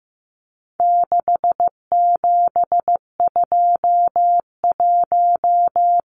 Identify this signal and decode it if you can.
Morse code, heard as a keyed tone, about 15 wpm, 700 Hz. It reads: 6721
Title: Numbers - 4-Digit 15wpm